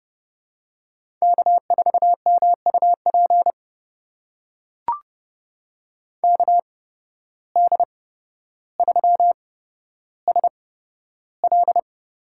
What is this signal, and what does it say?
Morse code, heard as a keyed tone, about 30 wpm, spaced as for 13 wpm, 700 Hz. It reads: K4MUP KD3SL
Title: USA Call Signs - 5 Characters - 8x Character Spacing - 30wpm